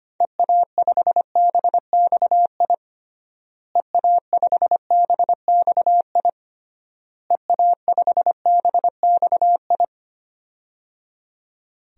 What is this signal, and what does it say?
Morse code, heard as a keyed tone, about 25 wpm, 700 Hz. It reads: EA5BXI EA5BXI EA5BXI
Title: DXCC-100 Call Signs - 6 Characters - Repeated 3x - 25wpm